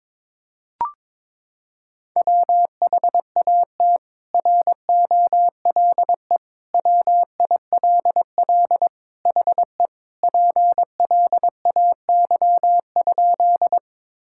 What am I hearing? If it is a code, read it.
Morse code, heard as a keyed tone, about 22 wpm, 700 Hz. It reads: WHAT ROLE WILL HE PLAY?
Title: Sentences from Top 1000 Words - Review Part 6 - 22wpm